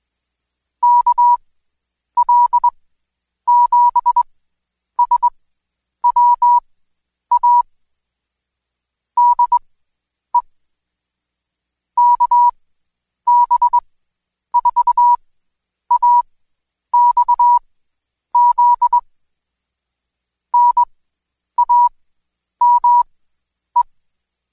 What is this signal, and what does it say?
Morse code, heard as a keyed tone, about 21 wpm, spaced as for 9 wpm, 970 Hz. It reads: KL7SWA DE KB4AXZ NAME
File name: Code Quick / AudioDownld2 / Seg116-Day21-DayDrill 21